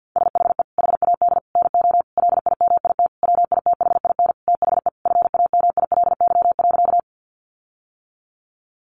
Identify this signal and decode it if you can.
Morse code, heard as a keyed tone, about 50 wpm, 700 Hz. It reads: SHE HAD NO LIMIT WITHIN THE FAMILY.
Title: Sentences from Top 1500 Words 50wpm